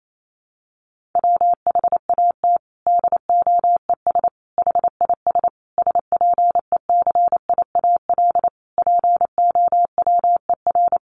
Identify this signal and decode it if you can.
Morse code, heard as a keyed tone, about 28 wpm, 700 Hz. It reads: WHAT DOES HIS SPECIAL POWER
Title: Sentences from Top 1400 Words 28wpm